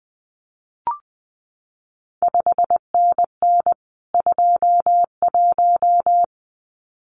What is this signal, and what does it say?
Morse code, heard as a keyed tone, about 20 wpm, 700 Hz. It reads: 5NN 21
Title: CQ WW DX Contest Exchange Encore - 20wpm